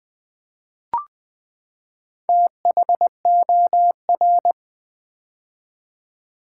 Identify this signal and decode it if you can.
Morse code, heard as a keyed tone, about 20 wpm, 700 Hz. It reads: THOR
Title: Build A Word - 4 Letters At A Time - 10-Letter Words - 20wpm